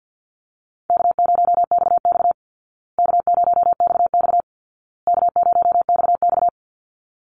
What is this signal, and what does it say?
Morse code, heard as a keyed tone, about 50 wpm, 700 Hz. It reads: K0XX K0XX K0XX
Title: Easy Call Signs Repeated 3x - 50wpm